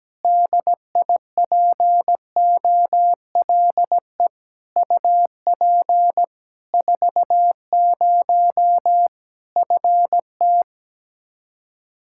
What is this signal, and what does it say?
Morse code, heard as a keyed tone, about 17 wpm, 700 Hz. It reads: DIPOLE UP 40 FT
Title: Practice QSO - Exchange 2 - Info - 17wpm